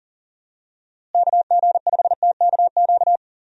Morse code, heard as a keyed tone, about 40 wpm, 700 Hz. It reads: KG5TKQ
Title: USA Call Signs - 6 Characters - Repeated 2x - 40wpm